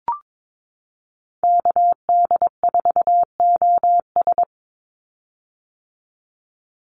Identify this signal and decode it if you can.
Morse code, heard as a keyed tone, about 22 wpm, 700 Hz. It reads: KD4OS
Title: Call Signs - 22wpm